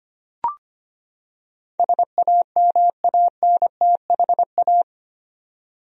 Morse code, heard as a keyed tone, about 25 wpm, 700 Hz. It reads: SAMANTHA
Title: USA Common Names - 25wpm